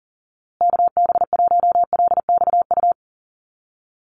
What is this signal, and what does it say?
Morse code, heard as a keyed tone, about 40 wpm, 700 Hz. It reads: KB1LXU
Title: USA Call Signs Repeated 3x 40wpm